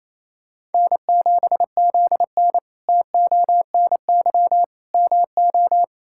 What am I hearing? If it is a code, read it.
Morse code, heard as a keyed tone, about 28 wpm, 700 Hz. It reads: N7ZN TONY MO
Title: K1USN SST QSO Practice - 28wpm